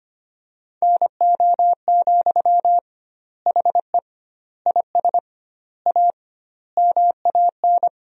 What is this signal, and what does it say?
Morse code, heard as a keyed tone, about 25 wpm, 700 Hz. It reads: NO, HE IS A MAN
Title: 2x Word-Spacing - Sentences from Top 400 Words - 25wpm